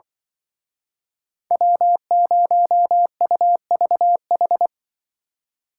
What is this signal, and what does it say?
Morse code, heard as a keyed tone, about 24 wpm, 700 Hz. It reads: W0UVH
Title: Speed Racing - USA 1x3 Call Signs 20wpm